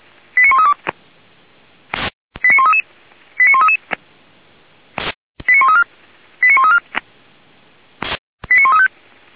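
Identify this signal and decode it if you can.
Five-tone selective call (ZVEI), heard as quick runs of five tones, tones of about 70 ms. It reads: 89132 89133 89133 89134 89134 89135
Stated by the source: ZVEI five-tone selective call 89131 received on SDR.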